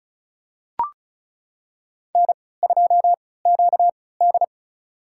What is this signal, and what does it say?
Morse code, heard as a keyed tone, about 36 wpm, spaced as for 20 wpm, 700 Hz. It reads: N2QD
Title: USA Call Signs - 3x Character Spacing - 35wpm